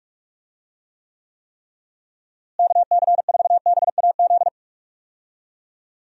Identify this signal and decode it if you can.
Morse code, heard as a keyed tone, about 46 wpm, 700 Hz. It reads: KC4BAZ
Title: USA Call Signs - 6 Characters - Repeated 3x - 45wpm